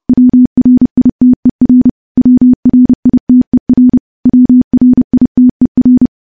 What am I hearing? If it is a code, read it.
Morse code, heard as a keyed tone, about 30 wpm, 250 Hz. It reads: WRITER WRITER WRITER
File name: morse / morse-training-30WPM